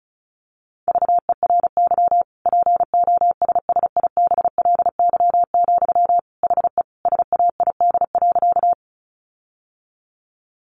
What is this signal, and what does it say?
Morse code, heard as a keyed tone, about 34 wpm, 700 Hz. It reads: VERY POSSIBLY, HE SAID.
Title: Sentences from Top 1300 Words 35wpm